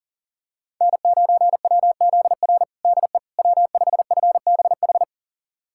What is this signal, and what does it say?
Morse code, heard as a keyed tone, about 37 wpm, 700 Hz. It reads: N9WZR DE W5FBH
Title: USA Callsign DE Callsign - 5 Characters - 40wpm